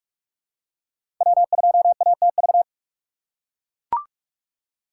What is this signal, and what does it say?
Morse code, heard as a keyed tone, about 44 wpm, 690 Hz. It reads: W2ATV
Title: USA Call Signs - 5 Characters - 2x Character Spacing - 45wpm